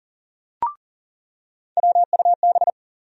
Morse code, heard as a keyed tone, about 38 wpm, 700 Hz. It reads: WUB
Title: USA Call Sign Postfixes - 3 Letters 40wpm